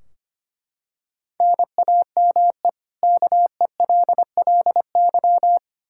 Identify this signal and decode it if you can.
Morse code, heard as a keyed tone, about 25 wpm, 700 Hz. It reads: NAME KELLY
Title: Practice QSO - Exchange 1 - Beginning - 25wpm